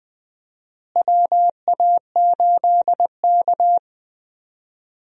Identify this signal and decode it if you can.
Morse code, heard as a keyed tone, about 20 wpm, 700 Hz. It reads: WA8K
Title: USA Call Signs - 4 Characters - Repeated 2x - 20wpm